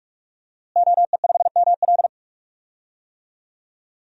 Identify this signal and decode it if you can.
Morse code, heard as a keyed tone, about 46 wpm, 700 Hz. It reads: OE5ML
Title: DXCC-100 Call Signs - 5 Characters - Repeated 3x - 45wpm